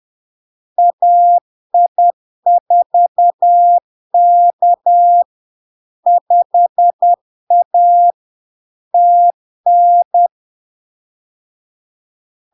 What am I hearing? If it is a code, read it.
Morse code, heard as a keyed tone, about 10 wpm, 700 Hz. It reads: AI4K 5A TN
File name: LARK-FD-2022-Log-in-CW-10wpm-700-Hz